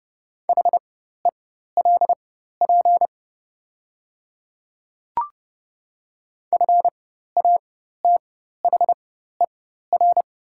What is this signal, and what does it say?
Morse code, heard as a keyed tone, about 30 wpm, spaced as for 14 wpm, 700 Hz. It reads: HELP FATHER
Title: 4x Character Spacing - Top 200 Words - 30wpm